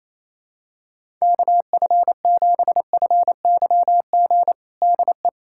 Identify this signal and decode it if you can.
Morse code, heard as a keyed tone, about 28 wpm, 700 Hz. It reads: KF7FYG DE
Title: USA Callsign DE Callsign 28wpm